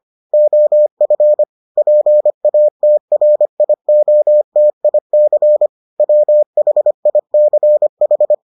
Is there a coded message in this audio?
Yes: Morse code from 0.3 to 8.3 s, heard as a keyed tone, about 25 wpm, 600 Hz. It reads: OF PATRIOTIC WHICH